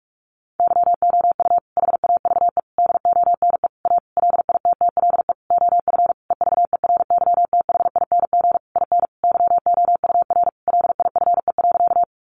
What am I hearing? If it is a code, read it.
Morse code, heard as a keyed tone, about 44 wpm, 700 Hz. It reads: YOU HAVE DONE A LITTLE OF EVERYTHING IN YOUR LIFE.
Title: Sentences from Top 700 Words 45wpm